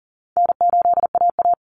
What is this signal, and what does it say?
Morse code, heard as a keyed tone, about 40 wpm, 700 Hz. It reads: N8AA
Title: Easy Call Signs Repeated 2x - 40wpm